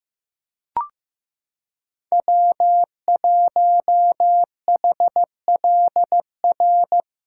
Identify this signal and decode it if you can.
Morse code, heard as a keyed tone, about 15 wpm, 700 Hz. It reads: W1HLR
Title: USA Callsign DE Callsign - 5 Characters - 15wpm